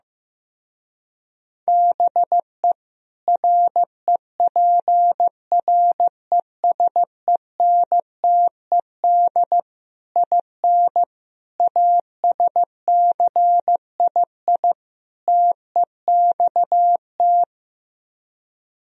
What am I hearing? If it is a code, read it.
Morse code, heard as a keyed tone, about 16 wpm, 700 Hz. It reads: BE REPRESENTED IN ASCII TEXT
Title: Speed-Racing - Sets of 5 Words 15wpm